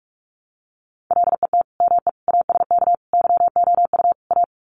Morse code, heard as a keyed tone, about 44 wpm, 700 Hz. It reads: LET ME ASK YOU A
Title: Sets of 5 Words Encore 45wpm